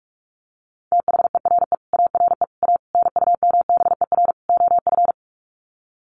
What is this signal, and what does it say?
Morse code, heard as a keyed tone, about 44 wpm, 700 Hz. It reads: THERE ARE A NUMBER OF
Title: Sets of 5 Words 45wpm